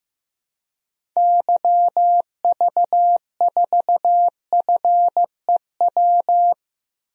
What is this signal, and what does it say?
Morse code, heard as a keyed tone, about 16 wpm, 700 Hz. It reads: YV4FEW
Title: DXCC-100 Call Signs - 6 Characters - 15wpm